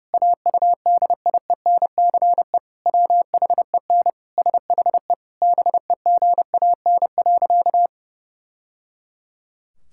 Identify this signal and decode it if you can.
Morse code, heard as a keyed tone, about 32 wpm, 700 Hz. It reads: AUDIENCE WHEN SHE BEGAN.
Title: Sentences from Top 1000 Words - Review Part 5 - 30wpm